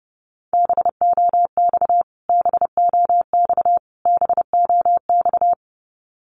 Morse code, heard as a keyed tone, about 30 wpm, 700 Hz. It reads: BOX BOX BOX
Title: Top 500 Words Repeated 30wpm